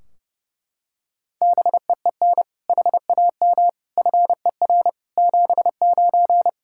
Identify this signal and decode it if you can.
Morse code, heard as a keyed tone, about 30 wpm, 700 Hz. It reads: BEEN HAM FER 79
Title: Practice QSO - Exchange 3 - More Info - 30wpm